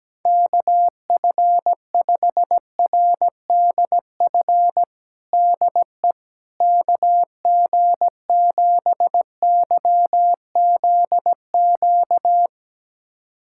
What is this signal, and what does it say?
Morse code, heard as a keyed tone, about 17 wpm, 700 Hz. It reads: KF5RDF DE KG7YZQ
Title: USA Callsign DE Callsign - 6 Characters - 17wpm